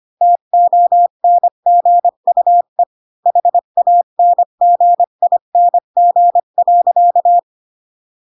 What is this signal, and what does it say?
Morse code, heard as a keyed tone, about 25 wpm, 700 Hz. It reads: TONGUE HANGING.